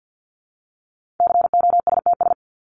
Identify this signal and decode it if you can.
Morse code, heard as a keyed tone, about 48 wpm, 700 Hz. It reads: COSTS
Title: ES:S Suffix - Top 100 Words with Base Words 50wpm